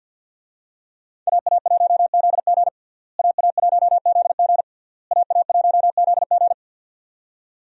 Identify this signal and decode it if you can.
Morse code, heard as a keyed tone, about 47 wpm, 700 Hz. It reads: AA1ZG AA1ZG AA1ZG
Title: USA Call Signs - 5 Characters - Repeated 3x - 50wpm